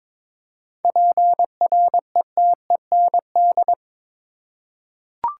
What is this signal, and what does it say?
Morse code, heard as a keyed tone, about 22 wpm, 700 Hz. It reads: PRETEND
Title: Seven Letter Words - 22wpm